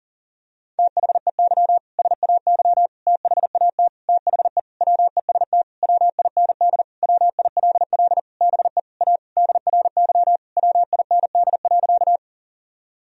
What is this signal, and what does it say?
Morse code, heard as a keyed tone, about 40 wpm, 700 Hz. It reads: THEY SAY THAT THE WEST WIND WILL BE A DRY WIND.
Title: Sentences from Top 1000 Words - Review Part 3 - 40wpm